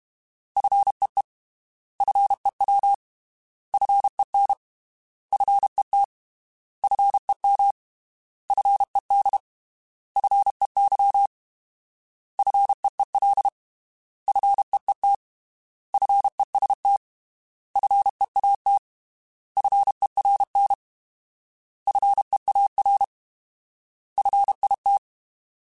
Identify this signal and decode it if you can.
Morse code, heard as a keyed tone, about 32 wpm, 800 Hz. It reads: FEE FEW FEN FET FEM FED FEY FEEL FEET FEST FEAT FERN FEAR FIT
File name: Alphabetical / Alphabetical mp3s / Fs @30